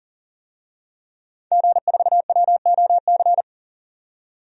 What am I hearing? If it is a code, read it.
Morse code, heard as a keyed tone, about 40 wpm, 700 Hz. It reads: G4WOC
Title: DXCC-100 Call Signs - 5 Characters - 40wpm